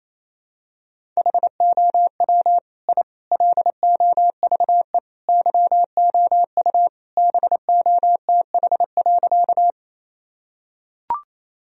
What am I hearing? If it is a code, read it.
Morse code, heard as a keyed tone, about 28 wpm, 700 Hz. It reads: HOW I LOVE YOU BOTH.
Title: Sentences from Top 1000 Words - Review Part 1 - 28wpm